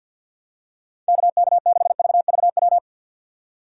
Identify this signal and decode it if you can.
Morse code, heard as a keyed tone, about 51 wpm, 700 Hz. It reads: KK6VVW
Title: USA Call Signs - 2x Character Spacing - 50wpm